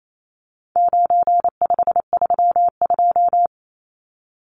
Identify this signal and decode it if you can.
Morse code, heard as a keyed tone, about 28 wpm, 700 Hz. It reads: 9532
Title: Numbers - 4-Digit 28wpm